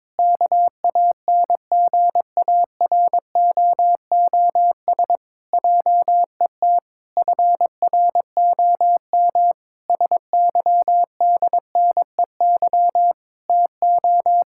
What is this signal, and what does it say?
Morse code, heard as a keyed tone, about 22 wpm, 700 Hz. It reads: KANGAROOS JET FROM SYDNEY TO
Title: Pangram Challenge v2 - 22wpm